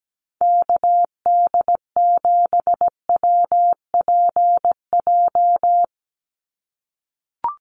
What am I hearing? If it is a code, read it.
Morse code, heard as a keyed tone, about 17 wpm, 700 Hz. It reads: KD7WPJ
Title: NA SOTA Activator Callsigns 17wpm